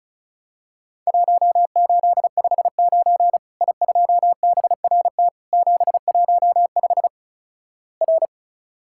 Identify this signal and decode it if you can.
Morse code, heard as a keyed tone, about 36 wpm, 700 Hz. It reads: 1859 I2BRT 715 R
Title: Worked All Europe DX Contest - QTC Exchange - 35wpm